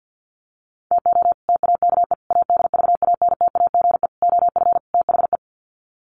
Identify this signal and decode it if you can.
Morse code, heard as a keyed tone, about 48 wpm, 700 Hz. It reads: TO TAKE ADVANTAGE OF THE
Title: Sets of 5 Words 50wpm